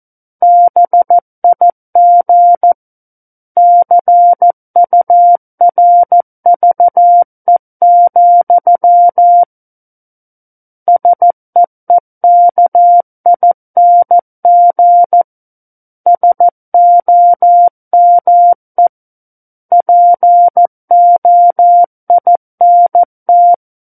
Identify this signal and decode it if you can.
Morse code, heard as a keyed tone, about 14 wpm, 700 Hz. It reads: BIG CURVE, SEEKING SOME POINT
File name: Book One / 05-Chapter 5 - 14 WPM